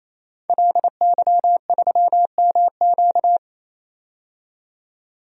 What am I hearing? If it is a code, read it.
Morse code, heard as a keyed tone, about 28 wpm, 700 Hz. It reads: LY3MQ
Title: DXCC-100 Call Signs - 5 Characters - 28wpm